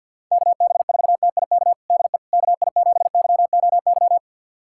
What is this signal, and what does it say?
Morse code, heard as a keyed tone, about 52 wpm, 700 Hz. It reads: KB4TIK DE KI7YOY
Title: USA Callsign DE Callsign - 6 Characters - 50wpm